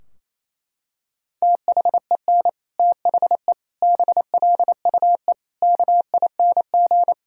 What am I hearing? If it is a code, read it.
Morse code, heard as a keyed tone, about 28 wpm, 700 Hz. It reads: THEN THE BLUE KING
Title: Sentences from Top 1000 Words - Review Part 3 - 28wpm